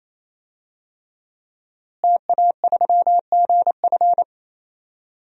Morse code, heard as a keyed tone, about 27 wpm, 700 Hz. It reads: TA3GF
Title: DXCC-100 Call Signs - 5 Characters - Repeated 3x - 28wpm